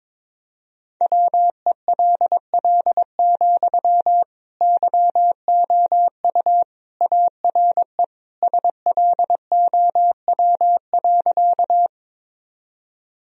Morse code, heard as a keyed tone, about 22 wpm, 700 Hz. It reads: WELL, YOU ARE SLOW.
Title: Sentences from Top 1000 Words - Review Part 2 - 22wpm